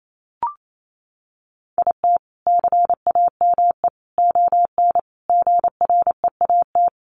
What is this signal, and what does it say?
Morse code, heard as a keyed tone, about 28 wpm, 700 Hz. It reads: IT CAME ON GREAT
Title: A Sound Of Thunder 28wpm